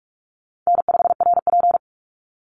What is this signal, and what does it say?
Morse code, heard as a keyed tone, about 44 wpm, 700 Hz. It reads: N5RP
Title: Call Signs - 45wpm